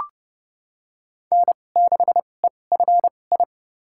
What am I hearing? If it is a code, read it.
Morse code, heard as a keyed tone, about 30 wpm, spaced as for 22 wpm, 700 Hz. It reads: N6EFI
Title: USA Call Signs - 5 Characters - 2x Character Spacing - 30wpm